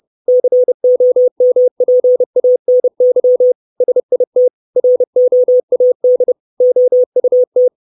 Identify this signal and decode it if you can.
Morse code, heard as a keyed tone, about 30 wpm, 500 Hz. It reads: COMPANY SIT ROAD OUT
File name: 632 Words 30WPM